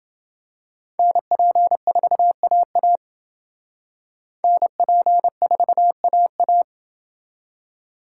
Morse code, heard as a keyed tone, about 29 wpm, 700 Hz. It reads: NP4AA NP4AA